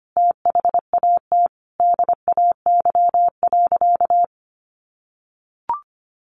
Morse code, heard as a keyed tone, about 25 wpm, 700 Hz. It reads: THAT DAY.
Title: Sentences from Top 1500 Words 25wpm